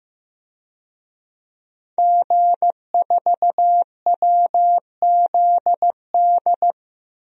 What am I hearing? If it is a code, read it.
Morse code, heard as a keyed tone, about 16 wpm, 700 Hz. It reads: G4WZD
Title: DXCC-100 Call Signs - 5 Characters - Repeated 2x - 15wpm